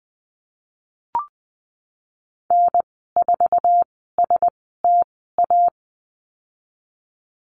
Morse code, heard as a keyed tone, about 20 wpm, spaced as for 15 wpm, 700 Hz. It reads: N4STA
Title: 048 Lesson 9g - Number 4 - Call Signs 20wpm